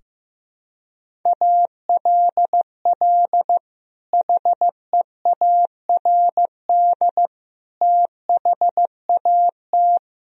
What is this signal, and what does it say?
Morse code, heard as a keyed tone, about 16 wpm, 700 Hz. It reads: ALL HEARD THAT
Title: Sentences from Top 1000 Words - Review Part 1 - 15wpm